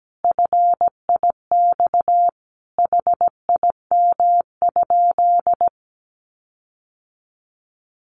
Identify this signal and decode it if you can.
Morse code, heard as a keyed tone, about 17 wpm, 700 Hz. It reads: FIX HIM?
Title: Sentences from Top 1400 Words 17wpm